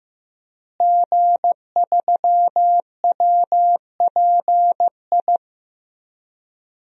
Morse code, heard as a keyed tone, about 15 wpm, 700 Hz. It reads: G3WPI
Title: DXCC-100 Call Signs - 5 Characters - Repeated 2x - 15wpm